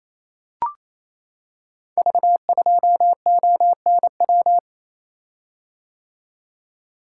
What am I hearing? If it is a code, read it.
Morse code, heard as a keyed tone, about 27 wpm, 700 Hz. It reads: V2ONW
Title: Code Groups - 5 Letter:Numbers - 28wpm